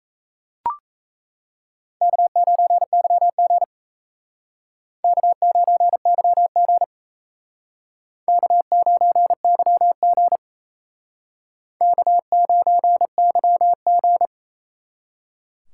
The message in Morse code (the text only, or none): K9YG K9YG K9YG K9YG